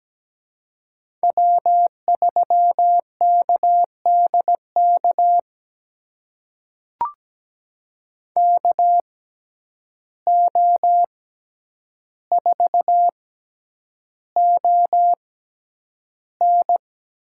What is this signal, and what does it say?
Morse code, heard as a keyed tone, about 17 wpm, spaced as for 9 wpm, 700 Hz. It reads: W3KDK KO4ON
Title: USA Call Signs - 5 Characters - 6x Character Spacing - 17wpm